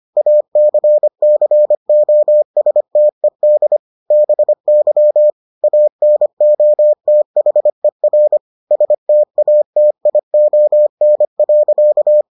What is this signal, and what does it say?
Morse code, heard as a keyed tone, about 25 wpm, 600 Hz. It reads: ACCOSTED BY ANOTHER STATION.